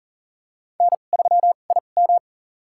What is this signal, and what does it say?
Morse code, heard as a keyed tone, about 40 wpm, spaced as for 29 wpm, 700 Hz. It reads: N3IM
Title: USA Call Signs - 4 Characters - 2x Character Spacing - 40wpm